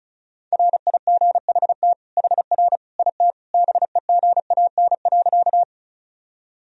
Morse code, heard as a keyed tone, about 36 wpm, 700 Hz. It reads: RIGHT HR IT BEGAN.
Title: Sentences from Top 300 Words and Abbreviations 35wpm